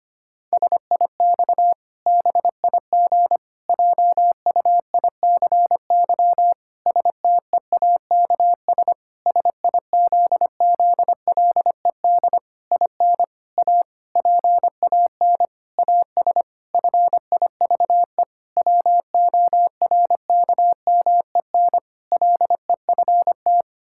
Morse code, heard as a keyed tone, about 25 wpm, 700 Hz. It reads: SIX BIG JUICY STEAKS SIZZLED IN A PAN AS FIVE WORKMEN LEFT
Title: Pangram Challenge v2 - 25wpm